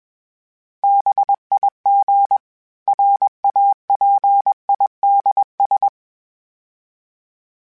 Morse code, heard as a keyed tone, about 21 wpm, 800 Hz. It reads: BIG RAPIDS
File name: fromSMs / FromSMs mp3s / Cities@20